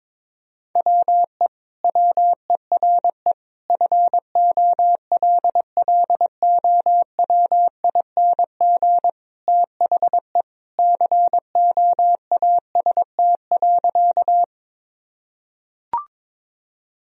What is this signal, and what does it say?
Morse code, heard as a keyed tone, about 22 wpm, 700 Hz. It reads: WE WERE FOLLOWING THE COAST.
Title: Sentences from Top 1000 Words - Review Part 4 - 22wpm